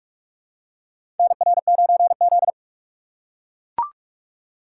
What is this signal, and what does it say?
Morse code, heard as a keyed tone, about 43 wpm, 700 Hz. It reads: NR9Z
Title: USA Call Signs - 4 Characters - 6x Character Spacing - 45wpm